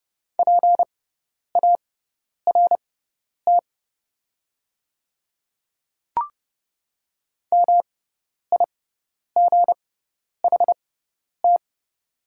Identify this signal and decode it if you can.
Morse code, heard as a keyed tone, about 30 wpm, spaced as for 10 wpm, 690 Hz. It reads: PART MIGHT
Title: 6x Character Spacing - Top 300 Words - 30wpm